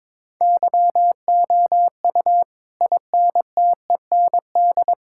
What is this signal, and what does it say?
Morse code, heard as a keyed tone, about 22 wpm, 700 Hz. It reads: YOU INTEND
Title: Sentences from Top 1600 Words 22wpm